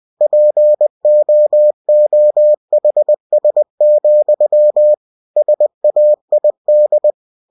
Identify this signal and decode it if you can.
Morse code, heard as a keyed tone, about 20 wpm, 600 Hz. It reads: POOHS, SAID
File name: cw-books / winnie-the-pooh / winnie-20wpm-page-0074